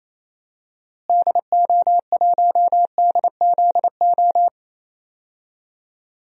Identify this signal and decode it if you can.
Morse code, heard as a keyed tone, about 28 wpm, 700 Hz. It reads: DO1DZO
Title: DXCC-100 Call Signs - 6 Characters - Repeated 2x - 28wpm